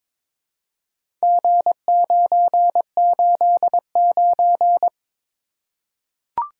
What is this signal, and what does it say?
Morse code, heard as a keyed tone, about 23 wpm, 700 Hz. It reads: G989
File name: practice / 22wpm / Four Letter-Number Combinations - 22wpm